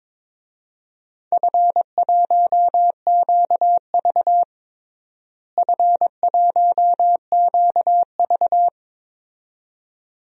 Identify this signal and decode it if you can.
Morse code, heard as a keyed tone, about 22 wpm, 700 Hz. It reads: F1QV F1QV
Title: DXCC-100 Call Signs - 4 Characters - Repeated 2x - 22wpm